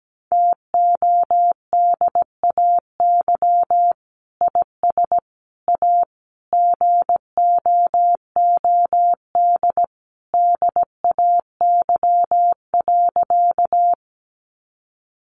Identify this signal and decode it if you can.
Morse code, heard as a keyed tone, about 17 wpm, 700 Hz. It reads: TODAY IS A GOOD DAY.
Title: Sentences from Top 600 Words 17wpm